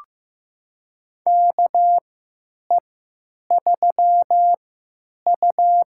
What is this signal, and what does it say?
Morse code, heard as a keyed tone, about 15 wpm, spaced as for 8 wpm, 700 Hz. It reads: KE3U
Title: USA Call Signs - 4 Characters - 3x Character Spacing - 15wpm